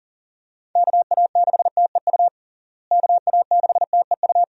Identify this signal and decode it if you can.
Morse code, heard as a keyed tone, about 40 wpm, 700 Hz. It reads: KA6TEU KA6TEU
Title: USA Call Signs - 6 Characters - Repeated 2x - 40wpm